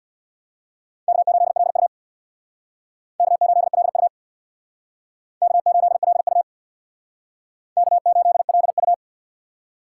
Morse code, heard as a keyed tone, about 62 wpm, 700 Hz. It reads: K8LU K8LU K8LU K8LU
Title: Speed Racing - USA 1x2 Call Signs 50wpm